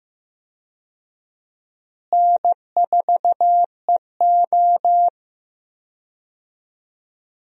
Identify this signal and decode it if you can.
Morse code, heard as a keyed tone, about 15 wpm, 700 Hz. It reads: N4EO
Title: USA Call Signs - 4 Characters - Repeated 2x - 15wpm